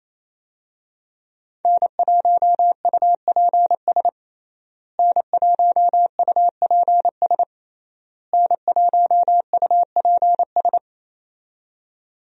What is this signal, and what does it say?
Morse code, heard as a keyed tone, about 28 wpm, 700 Hz. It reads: N1UPS N1UPS N1UPS
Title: USA Call Signs - 5 Characters - Repeated 3x - 28wpm